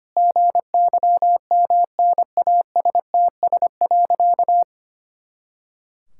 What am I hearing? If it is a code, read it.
Morse code, heard as a keyed tone, about 25 wpm, 700 Hz. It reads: GYMNASTS.
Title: Pangram Challenge v2 - 25wpm